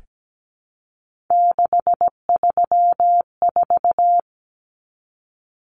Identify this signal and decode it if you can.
Morse code, heard as a keyed tone, about 17 wpm, 700 Hz. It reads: 634
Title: Common Cut Numbers - 3-Digit 17wpm